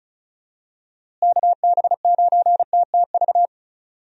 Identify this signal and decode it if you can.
Morse code, heard as a keyed tone, about 34 wpm, 700 Hz. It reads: KB9TTV
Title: USA Call Signs - 6 Characters - Repeated 3x - 35wpm